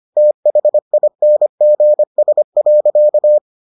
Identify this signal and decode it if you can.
Morse code, heard as a keyed tone, about 25 wpm, 600 Hz. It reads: THINGS.